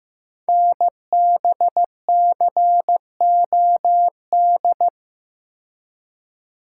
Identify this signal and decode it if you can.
Morse code, heard as a keyed tone, about 15 wpm, 700 Hz. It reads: NBCOD
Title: Code Groups - 5 Letters - 15wpm